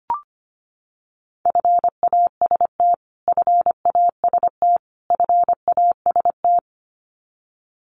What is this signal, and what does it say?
Morse code, heard as a keyed tone, about 25 wpm, 700 Hz. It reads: FAST FAST FAST
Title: Top 500 Words Repeated 25wpm